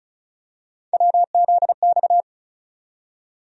Morse code, heard as a keyed tone, about 34 wpm, 700 Hz. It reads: WZX
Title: USA Call Sign Postfixes - 3 Letters 35wpm